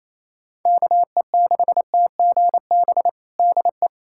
Morse code, heard as a keyed tone, about 28 wpm, 700 Hz. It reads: KE6TGB DE
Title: USA Callsign DE Callsign 28wpm